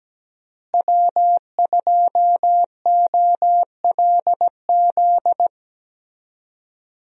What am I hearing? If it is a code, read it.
Morse code, heard as a keyed tone, about 17 wpm, 700 Hz. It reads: W2OLZ
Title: USA Call Signs - 5 Characters - Repeated 3x - 17wpm